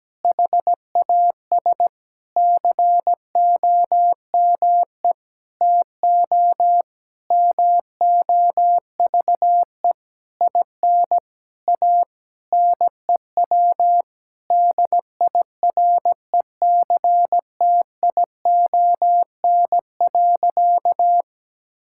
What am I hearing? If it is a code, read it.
Morse code, heard as a keyed tone, about 17 wpm, 700 Hz. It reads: HAS COME TO MOVE IN A NEW DIRECTION.
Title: Sentences from Top 1000 Words - Review Part 5 - 17wpm